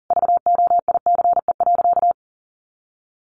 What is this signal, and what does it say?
Morse code, heard as a keyed tone, about 40 wpm, 700 Hz. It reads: VOICE.
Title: Sentences from Top 1500 Words 40wpm